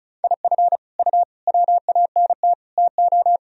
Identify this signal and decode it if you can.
Morse code, heard as a keyed tone, about 36 wpm, 700 Hz. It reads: IF U WANT TO
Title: Sentences from Top 200 Words and Abbreviations 35wpm